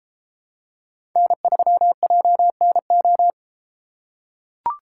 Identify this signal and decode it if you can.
Morse code, heard as a keyed tone, about 33 wpm, 700 Hz. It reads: N3JNO